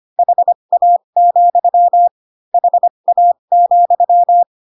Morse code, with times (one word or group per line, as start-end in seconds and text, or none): 0.2-2.1 HA,
2.5-4.4 HA,